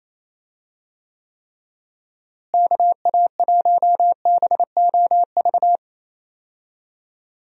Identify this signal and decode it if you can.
Morse code, heard as a keyed tone, about 27 wpm, 700 Hz. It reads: KA1BOV
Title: USA Call Signs - 6 Characters - Repeated 3x - 28wpm